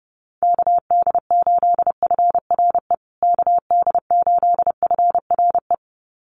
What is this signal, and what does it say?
Morse code, heard as a keyed tone, about 30 wpm, 700 Hz. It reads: KD8FRE KD8FRE
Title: USA Call Signs Repeated 2x 30wpm